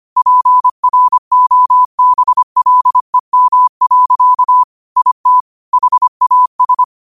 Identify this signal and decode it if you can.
Morse code, heard as a keyed tone, about 25 wpm, 1000 Hz. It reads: PROBLEM. IT HAS